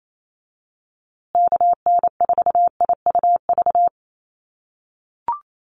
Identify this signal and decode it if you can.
Morse code, heard as a keyed tone, about 28 wpm, 700 Hz. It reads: KN4IUV
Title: USA Call Signs Repeated 3x 28wpm